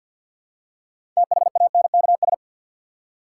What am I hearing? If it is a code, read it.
Morse code, heard as a keyed tone, about 48 wpm, 700 Hz. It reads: THANKS
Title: Six Letter Words - 50wpm